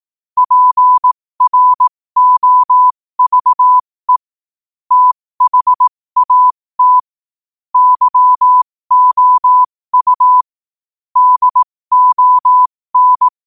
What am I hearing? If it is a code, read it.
Morse code, heard as a keyed tone, about 18 wpm, spaced as for 15 wpm, 1000 Hz. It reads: PROVE THAT YOU DON
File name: qotdfiles / 15wpm